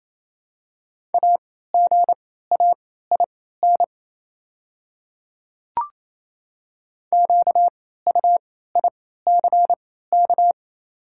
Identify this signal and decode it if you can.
Morse code, heard as a keyed tone, about 28 wpm, spaced as for 16 wpm, 700 Hz. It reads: AGAIN QUICK
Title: 3x Character Spacing - Top 500 Words - 28wpm